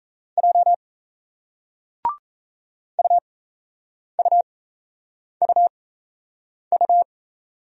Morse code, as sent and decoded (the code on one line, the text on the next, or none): .--- / ..- ..- ..- ..-
J UUUU